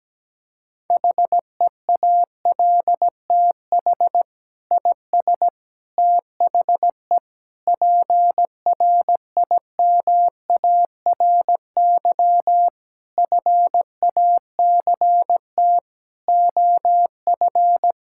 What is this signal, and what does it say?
Morse code, heard as a keyed tone, about 17 wpm, 700 Hz. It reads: HEALTH IS THE PRIMARY FACT OF
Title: Sentences from Top 1000 Words - Review Part 5 - 17wpm